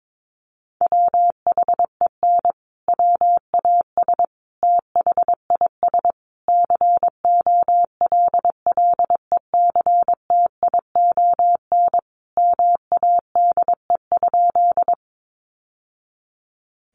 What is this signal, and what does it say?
Morse code, heard as a keyed tone, about 22 wpm, 700 Hz. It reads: WHEN WAS THIS COLLECTION MADE?
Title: Sentences from Top 1200 Words 22wpm